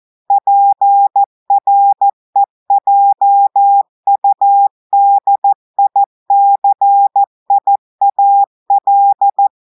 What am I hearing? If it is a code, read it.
Morse code, heard as a keyed tone, about 14 wpm, 800 Hz. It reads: PREJUDICIAL
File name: audios / portugais / portugais texte en clair 1 vitesse 14 mots minutes